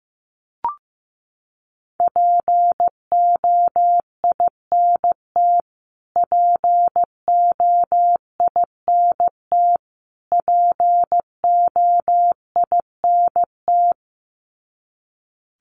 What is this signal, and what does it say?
Morse code, heard as a keyed tone, about 15 wpm, 700 Hz. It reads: POINT POINT POINT
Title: Top 300 Words Repeated 15wpm